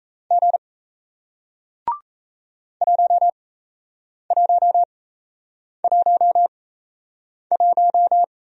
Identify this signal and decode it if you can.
Morse code, heard as a keyed tone, about 38 wpm, spaced as for 8 wpm, 700 Hz. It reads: G 1111